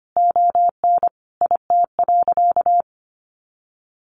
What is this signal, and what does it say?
Morse code, heard as a keyed tone, about 25 wpm, 700 Hz. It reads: ON IT.
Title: Additional Easy Phrases 25wpm